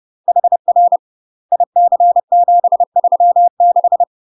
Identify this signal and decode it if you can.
Morse code, heard as a keyed tone, about 30 wpm, 700 Hz. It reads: HR IC736
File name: UK MASTER QSOs TRAINING / 30WPM / UK MASTER QSOs 0036